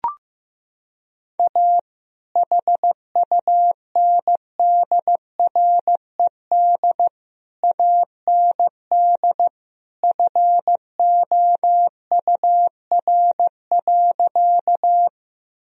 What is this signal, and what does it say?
Morse code, heard as a keyed tone, about 15 wpm, 700 Hz. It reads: A HUNDRED AND FOUR.
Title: Sentences from Top 1000 Words - Review Part 2 - 15wpm